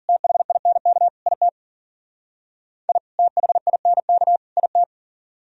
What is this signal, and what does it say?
Morse code, heard as a keyed tone, about 52 wpm, 700 Hz. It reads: THINK IT I THINK IT